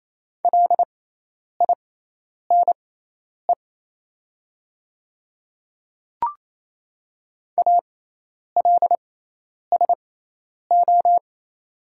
Morse code, heard as a keyed tone, about 27 wpm, spaced as for 10 wpm, 700 Hz. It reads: LINE ALSO
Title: 6x Character Spacing - Top 200 Words - 28wpm